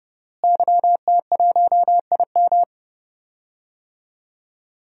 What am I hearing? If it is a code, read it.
Morse code, heard as a keyed tone, about 32 wpm, 700 Hz. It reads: YT1IM
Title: DXCC-100 Call Signs - 5 Characters - Repeated 3x - 30wpm